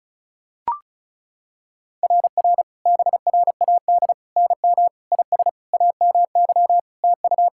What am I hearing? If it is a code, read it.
Morse code, heard as a keyed tone, about 34 wpm, 700 Hz. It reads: RR BRAD NM IS AMY TU
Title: SOTA QSO Practice 35wpm